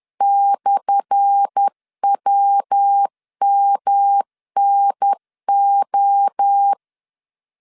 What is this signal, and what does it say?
Morse code, heard as a keyed tone, about 10 wpm, 800 Hz. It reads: /WMNO
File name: LEZIONI / 10 SIGNS / 10A SLASH 10wpm 2808